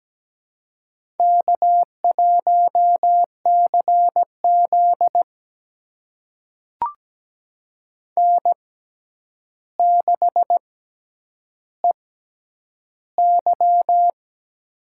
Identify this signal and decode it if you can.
Morse code, heard as a keyed tone, about 17 wpm, spaced as for 9 wpm, 700 Hz. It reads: K1CZ N6EY
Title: USA Call Signs - 4 Characters - 6x Character Spacing - 17wpm